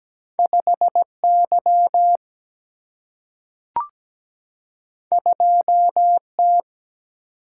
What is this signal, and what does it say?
Morse code, heard as a keyed tone, about 17 wpm, 700 Hz. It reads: 5Y 2T
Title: Single-Letter Single-Number 17wpm